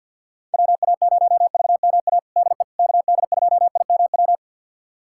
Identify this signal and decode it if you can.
Morse code, heard as a keyed tone, about 46 wpm, 690 Hz. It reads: WA0VMA DE KD2IMW
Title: USA Callsign DE Callsign 50wpm